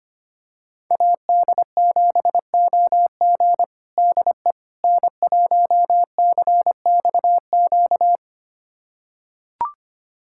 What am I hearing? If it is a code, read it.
Morse code, heard as a keyed tone, about 25 wpm, 700 Hz. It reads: AD7OG DE N1CXQ
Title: USA Callsign DE Callsign - 5 Characters - 25wpm